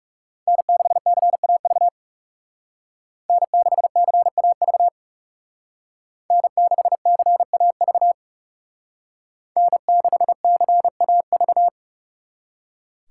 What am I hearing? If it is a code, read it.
Morse code, heard as a keyed tone, about 38 wpm, 700 Hz. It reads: N6CAV N6CAV N6CAV N6CAV